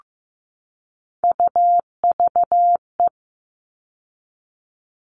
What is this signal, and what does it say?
Morse code, heard as a keyed tone, about 15 wpm, 700 Hz. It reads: UVE
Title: All Cut Numbers - 3-Digit 15wpm